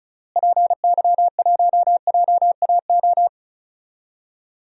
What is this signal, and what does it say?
Morse code, heard as a keyed tone, about 36 wpm, 700 Hz. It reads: PY1JAO
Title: DXCC-100 Call Signs - 6 Characters - Repeated 2x - 35wpm